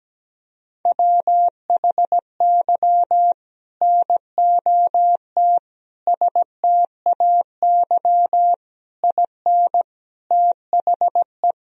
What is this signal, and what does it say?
Morse code, heard as a keyed tone, about 17 wpm, 700 Hz. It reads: WHY NOT STAY IN THE
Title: Sentences from Top 1000 Words - Review Part 1 - 17wpm